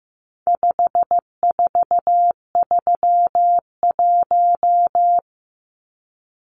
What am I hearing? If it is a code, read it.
Morse code, heard as a keyed tone, about 15 wpm, 700 Hz. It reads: 5431
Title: Numbers - 4-Digit 15wpm